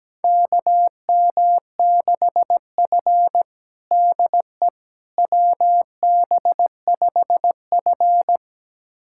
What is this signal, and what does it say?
Morse code, heard as a keyed tone, about 17 wpm, 700 Hz. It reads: KM6F DE WB5F
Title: USA Callsign DE Callsign - 4 Characters - 17wpm